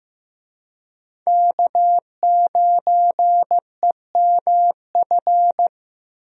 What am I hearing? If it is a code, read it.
Morse code, heard as a keyed tone, about 15 wpm, 700 Hz. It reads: K9EMF
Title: USA Call Signs - 5 Characters - Repeated 2x - 15wpm